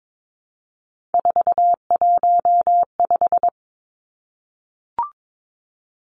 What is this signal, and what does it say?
Morse code, heard as a keyed tone, about 22 wpm, 700 Hz. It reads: 415
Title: Numbers - 3-Digit 22wpm